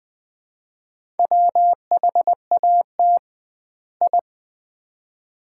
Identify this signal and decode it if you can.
Morse code, heard as a keyed tone, about 20 wpm, 700 Hz. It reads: WHAT I
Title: 2x Word-Spacing - Sets of 2 Words - 20wpm